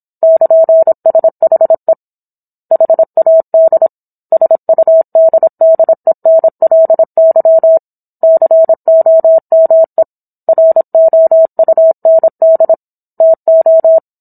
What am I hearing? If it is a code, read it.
Morse code, heard as a keyed tone, about 27 wpm, 650 Hz. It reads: <KN>SHE HAD SUDDENLY COME ROUND TO